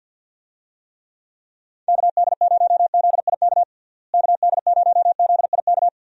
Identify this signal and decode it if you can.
Morse code, heard as a keyed tone, about 46 wpm, 700 Hz. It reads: KD0ZIK KD0ZIK
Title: USA Call Signs - 6 Characters - Repeated 3x - 50wpm